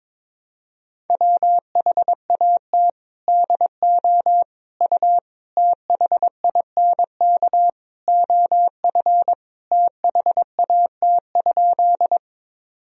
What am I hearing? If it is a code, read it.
Morse code, heard as a keyed tone, about 22 wpm, 700 Hz. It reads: WHAT DO U THINK OF THAT?
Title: Sentences from Top 200 Words and Abbreviations 22wpm